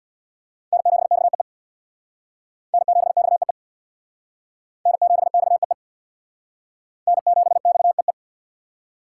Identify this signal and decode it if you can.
Morse code, heard as a keyed tone, about 66 wpm, 700 Hz. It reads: N7XEE N7XEE N7XEE N7XEE